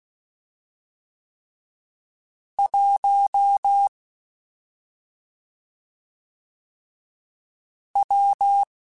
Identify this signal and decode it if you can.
Morse code, heard as a keyed tone, about 16 wpm, 800 Hz. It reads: 1 W
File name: Difficult Combos / Difficult Combos-mp3s / WPJ WPJ1 @5